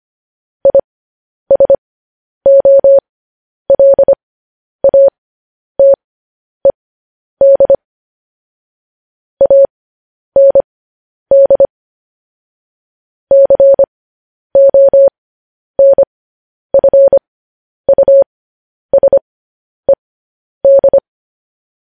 Morse code, heard as a keyed tone, about 25 wpm, spaced as for 10 wpm, 550 Hz. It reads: ISOLATED AND CONFUSED
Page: news-headlines-in-morse-code-at-10-wpm